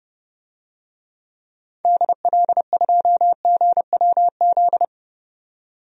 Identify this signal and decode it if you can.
Morse code, heard as a keyed tone, about 32 wpm, 700 Hz. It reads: DL2GWZ
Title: DXCC-100 Call Signs - 6 Characters - 30wpm